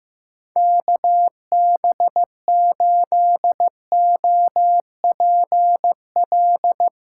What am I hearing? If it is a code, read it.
Morse code, heard as a keyed tone, about 15 wpm, 700 Hz. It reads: KB8OPL
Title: USA Call Signs - 6 Characters - 8x Character Spacing - 15wpm